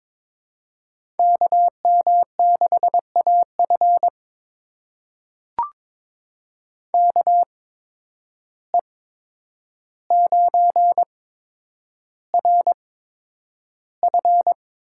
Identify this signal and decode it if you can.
Morse code, heard as a keyed tone, about 22 wpm, spaced as for 9 wpm, 700 Hz. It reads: KM6AF KE9RF
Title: USA Call Signs - 5 Characters - 8x Character Spacing - 22wpm